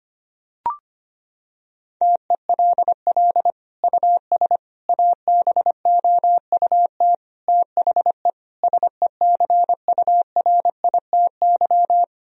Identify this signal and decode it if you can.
Morse code, heard as a keyed tone, about 25 wpm, 700 Hz. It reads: TELL US ABOUT THE SECURITY
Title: Sentences from Top 1000 Words - Review Part 4 - 25wpm